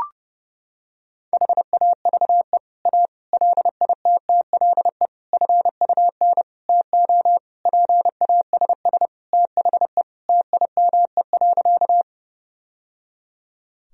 Morse code, heard as a keyed tone, about 30 wpm, 700 Hz. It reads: HAVE A LITTLE FUN TO PASS THE TIME.
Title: Sentences from Top 1000 Words - Review Part 4 - 30wpm